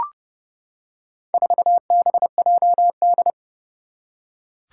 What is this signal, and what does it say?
Morse code, heard as a keyed tone, about 28 wpm, 700 Hz. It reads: 4BJD
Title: Four Challenging Letter-Number Combinations 30wpm